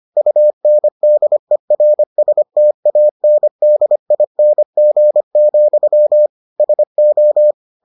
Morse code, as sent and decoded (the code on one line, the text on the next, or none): ..- -. -.. .-. ... - .- -. -.. .. -. --. --..-- / ... ---
UNDERSTANDING, SO